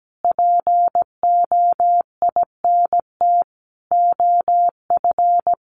Morse code, heard as a keyed tone, about 17 wpm, 700 Hz. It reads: POINT OF
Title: Sentences from Top 1300 Words 17wpm